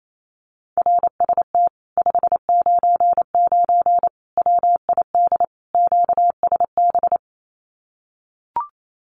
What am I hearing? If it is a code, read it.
Morse code, heard as a keyed tone, about 28 wpm, 700 Hz. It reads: RST 599 WID QSB
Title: Common QSO Phrases Encore 28wpm